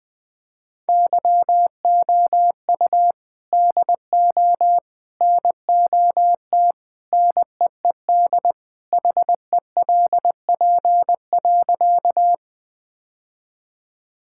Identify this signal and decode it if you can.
Morse code, heard as a keyed tone, about 21 wpm, 700 Hz. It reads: YOU DO NOT NEED HELP.
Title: Sentences with 3-Word Phrases - 20wpm